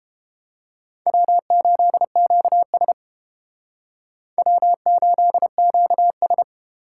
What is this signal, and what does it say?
Morse code, heard as a keyed tone, about 28 wpm, 700 Hz. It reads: W8QS W8QS